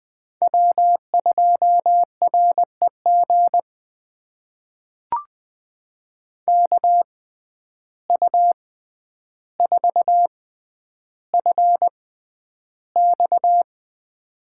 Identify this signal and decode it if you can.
Morse code, heard as a keyed tone, about 20 wpm, spaced as for 10 wpm, 700 Hz. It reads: W2REG KU4FX
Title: USA Call Signs - 5 Characters - 6x Character Spacing - 20wpm